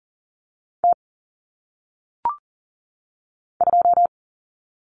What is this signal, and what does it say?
Morse code, heard as a keyed tone, about 40 wpm, 700 Hz. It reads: T 2
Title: Common Cut Numbers - 1-Digit 40wpm